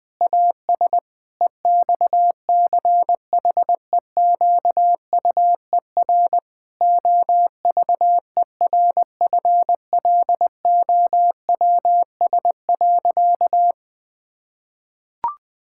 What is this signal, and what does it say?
Morse code, heard as a keyed tone, about 20 wpm, 700 Hz. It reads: AS EXCHEQUER OVERFLOWS.
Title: Pangram Challenge v2 - 20wpm